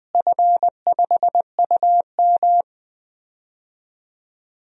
Morse code, heard as a keyed tone, about 21 wpm, 700 Hz. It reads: F5UM
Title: DXCC-100 Call Signs - 4 Characters - Repeated 2x - 20wpm